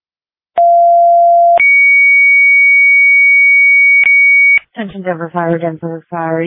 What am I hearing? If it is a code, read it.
Two-tone page, heard as a one-second tone, then a three-second tone, about 690 then 2250 Hz, with speech. It is heard from 0.6 to 4.6 s.